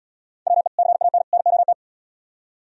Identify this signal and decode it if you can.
Morse code, heard as a keyed tone, about 70 wpm, 700 Hz. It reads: WE CAN NOT